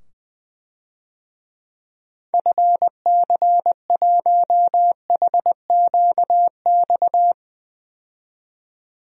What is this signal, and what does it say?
Morse code, heard as a keyed tone, about 20 wpm, 700 Hz. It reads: FC1HQX
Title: DXCC-100 Call Signs - 6 Characters - 20wpm